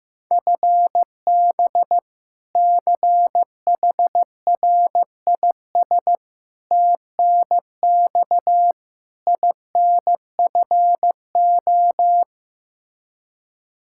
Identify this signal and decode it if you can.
Morse code, heard as a keyed tone, about 15 wpm, 700 Hz. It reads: FB CHRIS TNX INFO
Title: Practice QSO - Exchange 3 - More Info - 15wpm